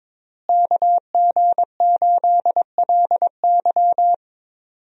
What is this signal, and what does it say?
Morse code, heard as a keyed tone, about 22 wpm, 700 Hz. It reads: KG8LY
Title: USA Call Signs - 5 Characters - Repeated 3x - 22wpm